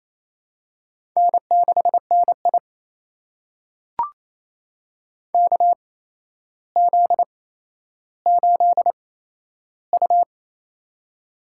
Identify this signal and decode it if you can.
Morse code, heard as a keyed tone, about 28 wpm, spaced as for 12 wpm, 700 Hz. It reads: N6NI KZ8U
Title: USA Call Signs - 4 Characters - 8x Character Spacing - 28wpm